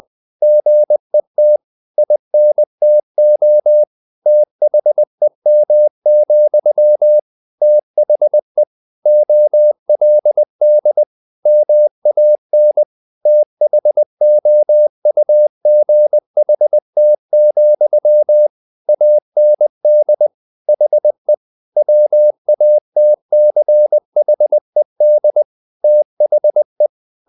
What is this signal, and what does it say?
Morse code, heard as a keyed tone, about 20 wpm, 600 Hz. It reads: GET INTO THEM, THE OLD MAN THOUGHT, AND HE WATCHED THE
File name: the old man and the sea / the old man and the sea part 1 20wpm